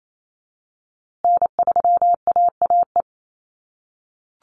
Morse code, heard as a keyed tone, about 28 wpm, 700 Hz. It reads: N3AAE
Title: USA Call Signs Repeated 3x 28wpm